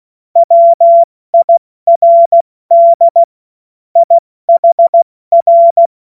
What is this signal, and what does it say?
Morse code, heard as a keyed tone, about 16 wpm, 680 Hz. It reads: WIRD IHR
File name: Morsekurs / Witze / Witz 02 April 2024 70 F80